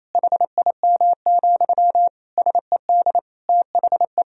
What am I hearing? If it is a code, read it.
Morse code, heard as a keyed tone, about 28 wpm, 700 Hz. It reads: HIM, SED THE
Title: Sentences from Top 400 Words and Abbreviations 28wpm